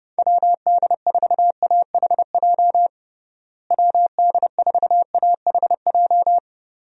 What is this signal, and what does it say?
Morse code, heard as a keyed tone, about 31 wpm, 700 Hz. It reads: WD4AHJ WD4AHJ
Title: USA Call Signs - 6 Characters - Repeated 2x - 30wpm